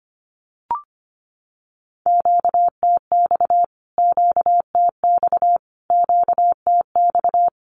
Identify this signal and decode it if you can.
Morse code, heard as a keyed tone, about 25 wpm, 700 Hz. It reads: QTX QTX QTX
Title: Q-Codes Repeated 3x 25wpm